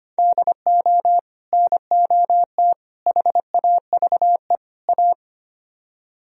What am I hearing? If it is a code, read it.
Morse code, heard as a keyed tone, about 30 wpm, 700 Hz. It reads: DO NOT HAVE A
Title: Speed-Racing - Sets of 4 Words 25wpm